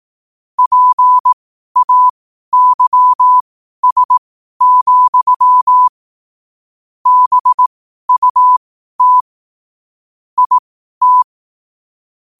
Morse code, heard as a keyed tone, about 18 wpm, spaced as for 12 wpm, 1000 Hz. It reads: PAYS, BUT IT
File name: qotdfiles / 12wpm